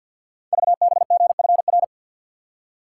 Morse code, heard as a keyed tone, about 50 wpm, 690 Hz. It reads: VBGFR
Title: Code Groups - 5 Letters - 50wpm